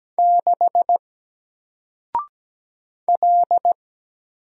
Morse code, heard as a keyed tone, about 17 wpm, 700 Hz. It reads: 6 L
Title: Single Letter-Number Alt 17wpm